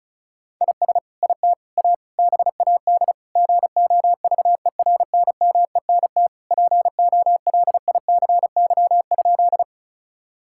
Morse code, heard as a keyed tone, about 34 wpm, 700 Hz. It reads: IS IT A BAD GOVERNMENT POLICY?
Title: Sentences from Top 1000 Words - Review Part 5 - 35wpm